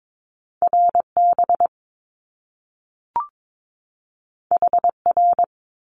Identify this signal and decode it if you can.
Morse code, heard as a keyed tone, about 22 wpm, 700 Hz. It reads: RB HR
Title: Top Two Letter Combinations in Words 22wpm